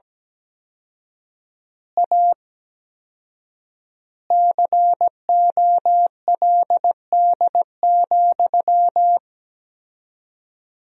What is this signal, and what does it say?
Morse code, heard as a keyed tone, about 17 wpm, 700 Hz. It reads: A COLD,
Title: 4x Word-Spacing - Sentences from Top 500 Words - 17wpm